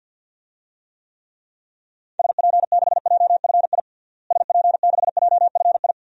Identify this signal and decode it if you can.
Morse code, heard as a keyed tone, about 51 wpm, 700 Hz. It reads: SP6JFI SP6JFI
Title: DXCC-100 Call Signs - 6 Characters - Repeated 2x - 50wpm